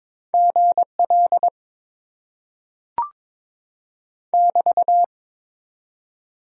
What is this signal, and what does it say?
Morse code, heard as a keyed tone, about 22 wpm, 700 Hz. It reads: GL <BT>
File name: Top 90 QSO Elements - 6x Character Spacing - 22wpm